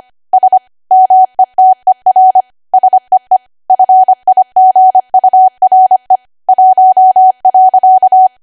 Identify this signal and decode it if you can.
Morse code, heard as a keyed tone, about 25 wpm, 750 Hz. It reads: S METER SEE FIGURE 1.
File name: Archive / 25 WPM / 25WPM